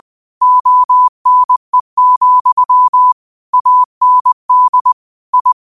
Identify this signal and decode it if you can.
Morse code, heard as a keyed tone, about 20 wpm, 1000 Hz. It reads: ONE, AND I
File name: 20wpm